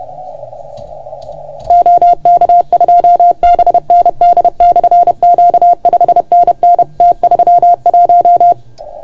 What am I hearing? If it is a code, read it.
Morse code, heard as a keyed tone, about 30 wpm, 690 Hz. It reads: OK2BND/Q 5NN T31
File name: Archiv / CUC381 / OK1FGD CUC 381